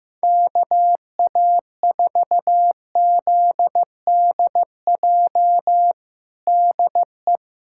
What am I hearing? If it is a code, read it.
Morse code, heard as a keyed tone, about 15 wpm, 700 Hz. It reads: KA4ZDJ DE
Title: USA Callsign DE Callsign - 6 Characters - 15wpm